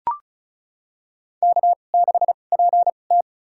Morse code, heard as a keyed tone, about 36 wpm, spaced as for 25 wpm, 690 Hz. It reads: K6PT
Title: USA Call Signs - 4 Characters - 2x Character Spacing - 35wpm